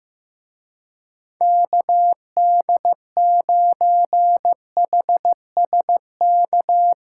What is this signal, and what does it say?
Morse code, heard as a keyed tone, about 16 wpm, 700 Hz. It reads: KD9HSK
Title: Speed Racing - USA 2x3 Call Signs 15wpm